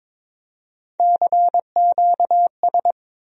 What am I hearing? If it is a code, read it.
Morse code, heard as a keyed tone, about 22 wpm, 700 Hz. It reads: CQS
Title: USA Call Sign Postfixes - 3 Letters 22wpm